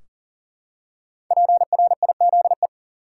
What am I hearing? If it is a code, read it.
Morse code, heard as a keyed tone, about 40 wpm, 700 Hz. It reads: PRIZE
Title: Five Letter Words - 40wpm